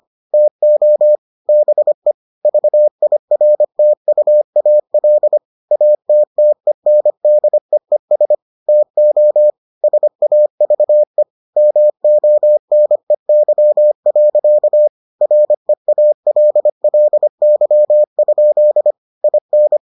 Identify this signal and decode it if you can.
Morse code, heard as a keyed tone, about 25 wpm, 600 Hz. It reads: TO BE VIRTUAL ATTENDEES TO SAVE MONEY. REALLY? IN